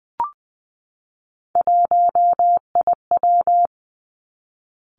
Morse code, heard as a keyed tone, about 19 wpm, 700 Hz. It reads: 1IW
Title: Three Challenging Letter-Number Combinations 20wpm